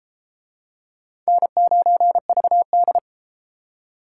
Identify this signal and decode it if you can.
Morse code, heard as a keyed tone, about 32 wpm, 700 Hz. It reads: N9VD